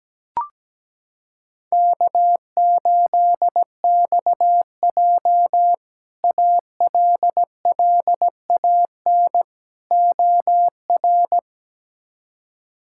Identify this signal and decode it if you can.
Morse code, heard as a keyed tone, about 17 wpm, 700 Hz. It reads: K8XJ ALLAN OR
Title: K1USN SST QSO Practice - 17wpm